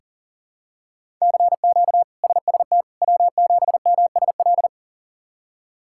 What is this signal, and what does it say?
Morse code, heard as a keyed tone, about 37 wpm, 700 Hz. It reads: CQ SST W7MSL
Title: K1USN SST QSO Practice - 40wpm